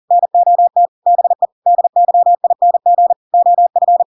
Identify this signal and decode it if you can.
Morse code, heard as a keyed tone, about 40 wpm, 700 Hz. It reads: NOT BE DYING OF
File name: Pinocchio / Pinocchio ch5 40 40 ik1hge